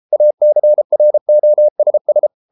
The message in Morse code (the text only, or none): ACROSS